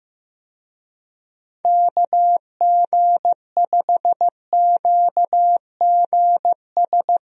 Morse code, heard as a keyed tone, about 15 wpm, 700 Hz. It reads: KG5QGS
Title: USA Call Signs - 6 Characters - Repeated 3x - 15wpm